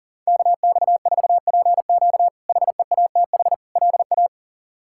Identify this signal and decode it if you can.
Morse code, heard as a keyed tone, about 40 wpm, 700 Hz. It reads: KX4PQ HEATH LA
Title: K1USN SST QSO Practice - 40wpm